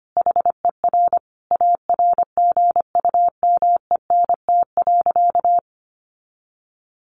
Morse code, heard as a keyed tone, about 25 wpm, 700 Hz. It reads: HER ARGUMENT.
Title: Sentences from Top 1400 Words 25wpm